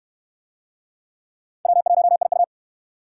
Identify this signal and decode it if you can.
Morse code, heard as a keyed tone, about 68 wpm, 700 Hz. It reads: W1IU